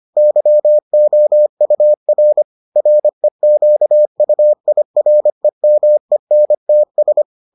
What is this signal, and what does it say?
Morse code, heard as a keyed tone, about 25 wpm, 600 Hz. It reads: YOUR REQUIREMENTS